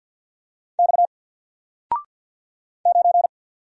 Morse code, heard as a keyed tone, about 48 wpm, 700 Hz. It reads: X 9
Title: Single Character (Letter:Number:Punctuation:Prosign) Alt 50wpm